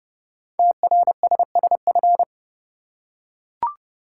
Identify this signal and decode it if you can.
Morse code, heard as a keyed tone, about 30 wpm, 700 Hz. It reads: TRSSF
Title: Code Groups - 5 Letters - 30wpm